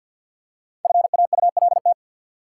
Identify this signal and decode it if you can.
Morse code, heard as a keyed tone, about 47 wpm, 700 Hz. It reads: FAULT
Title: Five Letter Words - 50wpm